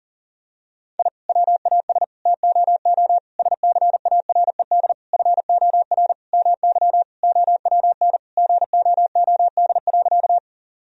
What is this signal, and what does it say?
Morse code, heard as a keyed tone, about 37 wpm, 700 Hz. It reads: I WAS TOO SCARED FOR MY OWN GOOD.
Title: Sentences from Top 1000 Words - Review Part 4 - 40wpm